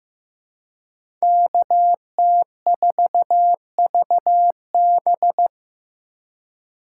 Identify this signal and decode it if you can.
Morse code, heard as a keyed tone, about 15 wpm, 700 Hz. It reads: KT4VB
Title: USA Call Signs - 5 Characters - 2x Character Spacing - 15wpm